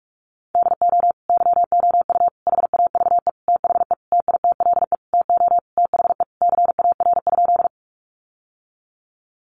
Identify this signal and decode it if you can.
Morse code, heard as a keyed tone, about 44 wpm, 700 Hz. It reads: DO YOU HAVE THE TITLE TO THE CAR?
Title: Sentences from Top 1200 Words 45wpm